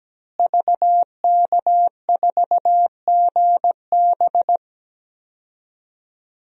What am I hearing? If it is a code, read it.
Morse code, heard as a keyed tone, about 17 wpm, 700 Hz. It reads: VK4GB
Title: DXCC-100 Call Signs - 5 Characters - Repeated 2x - 17wpm